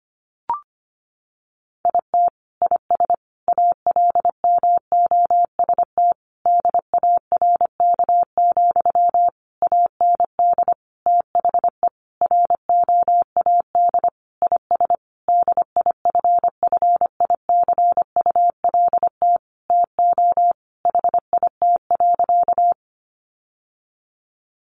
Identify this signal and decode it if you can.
Morse code, heard as a keyed tone, about 25 wpm, 700 Hz. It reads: IT IS ALMOST DARK, AND THE ROAD IS DIFFICULT TO HIT.
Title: Sentences from Top 1200 Words 25wpm